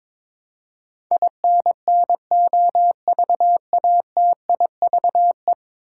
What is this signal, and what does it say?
Morse code, heard as a keyed tone, about 22 wpm, 700 Hz. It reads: INNOVATIVE
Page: morse-code-ninja-practice-22wpm